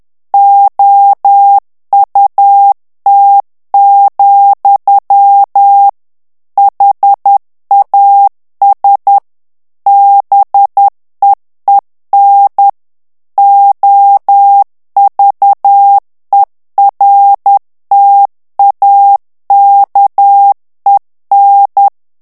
Morse code, heard as a keyed tone, about 10 wpm, 800 Hz. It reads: OUT, HAS BEEN OVERTAKEN
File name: ANGLAIS texte clair 3 vitesse 10 mots minute